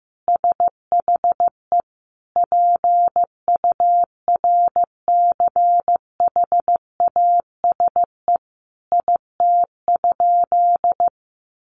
Morse code, heard as a keyed tone, about 15 wpm, 700 Hz. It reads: SHE PURCHASE IT?
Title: Sentences from Top 800 Words 15wpm